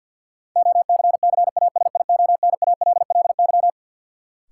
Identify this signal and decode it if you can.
Morse code, heard as a keyed tone, about 52 wpm, 700 Hz. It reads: OCCASIONALLY
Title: Sentences from Top 1300 Words - New Word Drill Alt - 50wpm